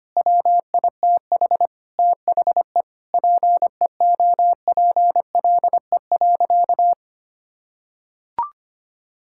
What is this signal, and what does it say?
Morse code, heard as a keyed tone, about 25 wpm, 700 Hz. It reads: WITH THE PEOPLE.
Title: Sentences from Top 1000 Words - Review Part 5 - 25wpm